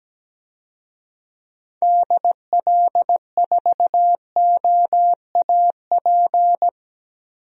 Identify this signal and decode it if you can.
Morse code, heard as a keyed tone, about 17 wpm, 700 Hz. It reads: DL4OAP
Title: USA Call Signs - 6 Characters - Repeated 3x - 17wpm